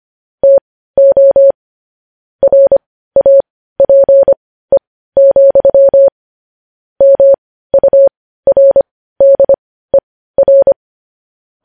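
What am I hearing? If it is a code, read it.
Morse code, heard as a keyed tone, about 25 wpm, spaced as for 15 wpm, 550 Hz. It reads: TO RAPE, MURDER
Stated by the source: Morse code news headlines broadcast.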